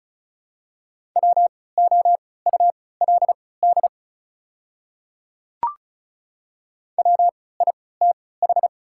Morse code, heard as a keyed tone, about 36 wpm, spaced as for 20 wpm, 700 Hz. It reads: WOULD WITH
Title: 3x Character Spacing - Top 200 Words - 35wpm